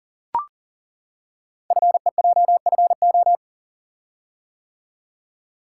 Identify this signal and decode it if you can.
Morse code, heard as a keyed tone, about 38 wpm, 700 Hz. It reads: FEJFO
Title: Code Groups - 5 Letters - 40wpm